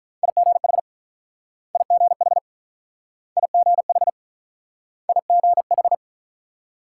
Morse code, heard as a keyed tone, about 45 wpm, 700 Hz. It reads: IGH IGH IGH IGH